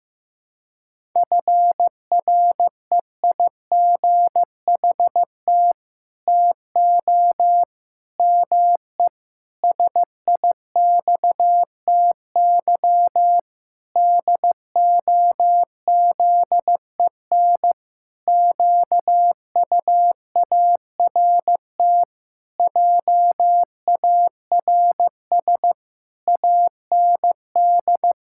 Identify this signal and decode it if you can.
Morse code, heard as a keyed tone, about 15 wpm, 700 Hz. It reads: FREIGHT TO ME SIXTY DOZEN QUART JARS AND
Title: Pangram Challenge v2 - 15wpm